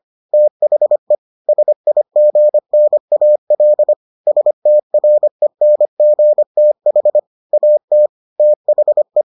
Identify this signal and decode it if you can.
Morse code, heard as a keyed tone, about 25 wpm, 600 Hz. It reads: THE SIGNAL STRENGTH AT THE